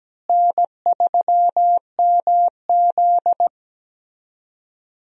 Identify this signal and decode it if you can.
Morse code, heard as a keyed tone, about 17 wpm, 700 Hz. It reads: N3MZ
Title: USA Call Signs - 4 Characters - Repeated 2x - 17wpm